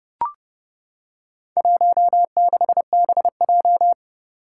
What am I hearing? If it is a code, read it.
Morse code, heard as a keyed tone, about 30 wpm, 700 Hz. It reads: 16BJ
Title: Four Challenging Letter-Number Combinations 30wpm